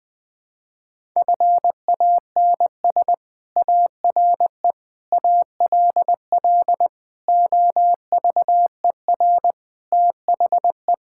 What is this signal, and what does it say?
Morse code, heard as a keyed tone, about 20 wpm, 700 Hz. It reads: FANS ARE ALL OVER THE
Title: Sentences with 4-Word Phrases - 20wpm